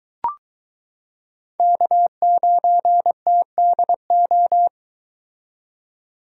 Morse code, heard as a keyed tone, about 23 wpm, 700 Hz. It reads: K9TDO